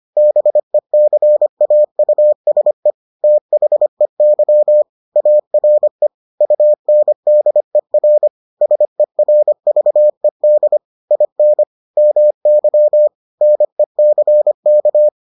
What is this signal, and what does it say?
Morse code, heard as a keyed tone, about 25 wpm, 600 Hz. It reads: BECAUSE THEY ARE UNDER SERVED IN MY NECK